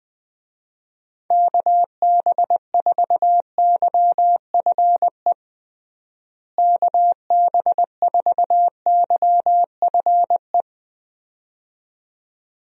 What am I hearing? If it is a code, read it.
Morse code, heard as a keyed tone, about 20 wpm, 700 Hz. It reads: KB4YFE KB4YFE
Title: USA Call Signs - 6 Characters - Repeated 2x - 20wpm